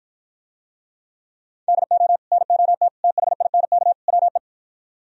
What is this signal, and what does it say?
Morse code, heard as a keyed tone, about 52 wpm, 700 Hz. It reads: DO NOT THINK WE